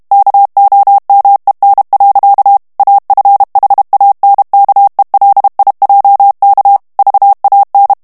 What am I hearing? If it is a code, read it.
Morse code, heard as a keyed tone, about 32 wpm, 800 Hz. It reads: KOMEN. AFHANKELIJK VAN
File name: morse mp3 / HOLLANDAIS texte clair 1 vitesse 30 mots minute